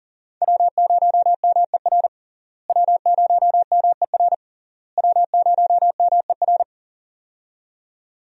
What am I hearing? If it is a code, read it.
Morse code, heard as a keyed tone, about 40 wpm, 700 Hz. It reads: W0MER W0MER W0MER
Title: USA Call Signs - 5 Characters - Repeated 3x - 40wpm